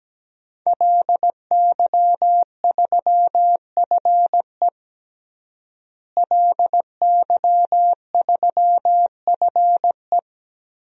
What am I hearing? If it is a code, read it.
Morse code, heard as a keyed tone, about 17 wpm, 700 Hz. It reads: LY3FE LY3FE
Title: DXCC-100 Call Signs - 5 Characters - Repeated 2x - 17wpm